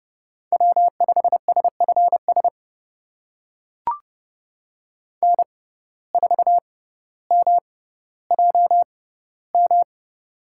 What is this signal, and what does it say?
Morse code, heard as a keyed tone, about 31 wpm, spaced as for 16 wpm, 700 Hz. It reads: W5SFS N4MJM
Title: USA Call Signs - 6x Character Spacing - 30wpm